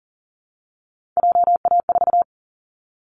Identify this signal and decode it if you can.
Morse code, heard as a keyed tone, about 40 wpm, 700 Hz. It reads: JA4
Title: Callsign Prefixes for Top 25 DXCC - Repeated 3x - 40wpm